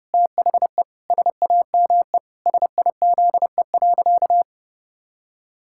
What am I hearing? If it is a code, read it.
Morse code, heard as a keyed tone, about 30 wpm, 700 Hz. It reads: THE SAME SIZE.
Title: Pangram Challenge v2 - 30wpm